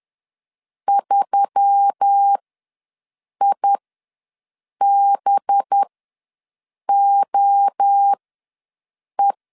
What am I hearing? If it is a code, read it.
Morse code, heard as a keyed tone, about 10 wpm, spaced as for 6 wpm, 800 Hz. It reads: 3IBOE
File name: LEZIONI / 7 GF9Z / 7 GF9 ws 10wpm 4034